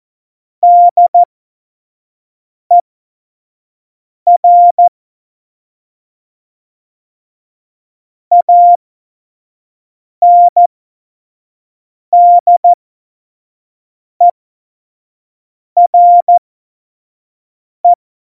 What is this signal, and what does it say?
Morse code, heard as a keyed tone, about 14 wpm, spaced as for 5 wpm, 700 Hz. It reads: DER ANDERE
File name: Morsekurs / Witze / Witz 28. August 2018, 25 F70